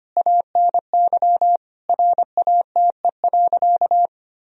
Morse code, heard as a keyed tone, about 25 wpm, 700 Hz. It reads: ANY RATE.
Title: Sentences from Top 1000 Words - Review Part 4 - 25wpm